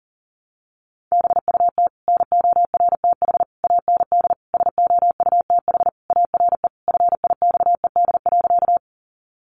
Morse code, heard as a keyed tone, about 40 wpm, 700 Hz. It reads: BUT NORTH AND SOUTH ARE FIXED.
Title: Sentences from Top 1100 Words 40wpm